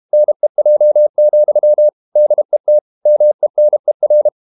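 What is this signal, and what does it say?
Morse code, heard as a keyed tone, about 32 wpm, 600 Hz. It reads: NEJ, DET MENER
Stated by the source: High-speed QRQ CW at 32 WPM.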